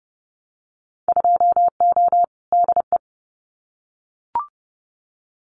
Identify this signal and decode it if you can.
Morse code, heard as a keyed tone, about 30 wpm, 700 Hz. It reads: 2O DE
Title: Winter Field Day Exchanges Encore 30wpm